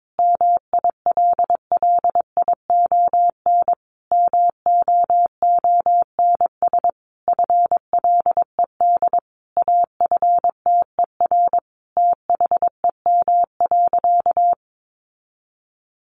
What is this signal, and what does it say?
Morse code, heard as a keyed tone, about 22 wpm, 700 Hz. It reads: MILLION MOONS FLED AFTER THEM.
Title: A Sound Of Thunder 22wpm